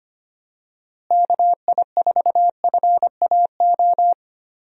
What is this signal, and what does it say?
Morse code, heard as a keyed tone, about 25 wpm, 700 Hz. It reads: KI4FAO
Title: USA Call Signs - 6 Characters - Repeated 3x - 25wpm